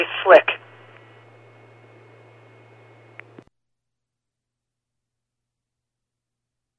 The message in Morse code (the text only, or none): none